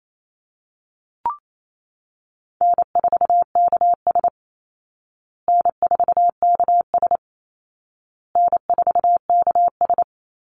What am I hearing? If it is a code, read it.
Morse code, heard as a keyed tone, about 28 wpm, 700 Hz. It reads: N4KS N4KS N4KS
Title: Easy Call Signs Repeated 3x - 28wpm